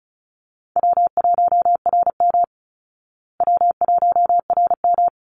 Morse code, heard as a keyed tone, about 34 wpm, 700 Hz. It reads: W1RM W1RM
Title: Easy Call Signs Repeated 2x - 35wpm